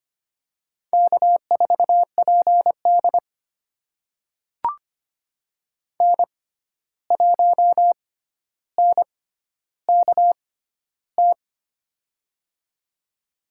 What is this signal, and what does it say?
Morse code, heard as a keyed tone, about 25 wpm, spaced as for 12 wpm, 700 Hz. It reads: K4PD N1NKT
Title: USA Call Signs - 6x Character Spacing - 25wpm